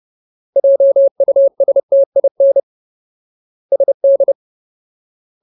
Morse code, heard as a keyed tone, about 30 wpm, 550 Hz. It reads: JUSTIN SD
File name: Ultimate Word List / 30WPM Ultimate